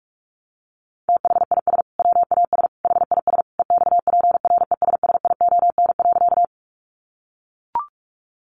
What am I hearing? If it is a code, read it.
Morse code, heard as a keyed tone, about 44 wpm, 700 Hz. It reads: THIS WAS HIS EXPRESSION.
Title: Sentences from Top 1200 Words 45wpm